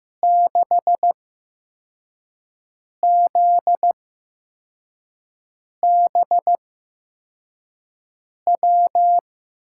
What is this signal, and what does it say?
Morse code, heard as a keyed tone, about 15 wpm, spaced as for 4 wpm, 700 Hz. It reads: 6ZBW
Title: USA Call Signs - 6 Characters - 8x Character Spacing - 15wpm